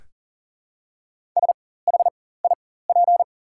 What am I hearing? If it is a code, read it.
Morse code, heard as a keyed tone, about 40 wpm, spaced as for 19 wpm, 700 Hz. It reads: SHIP
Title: 4x Character Spacing - Top 500 Words - 40wpm